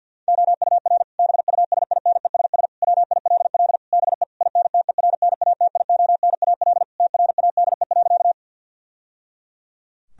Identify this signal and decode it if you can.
Morse code, heard as a keyed tone, about 50 wpm, 700 Hz. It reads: OUR BUSINESS WILL BE INTERNATIONAL TRADE.
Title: Sentences from Top 1000 Words - Review Part 4 - 50wpm